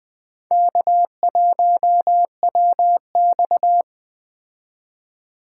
Morse code, heard as a keyed tone, about 20 wpm, 700 Hz. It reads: K1WX